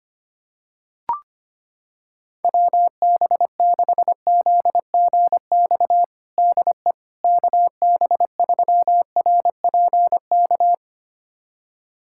Morse code, heard as a keyed tone, about 25 wpm, 700 Hz. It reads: WB6ZGX DE KB3RPK
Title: USA Callsign DE Callsign 25wpm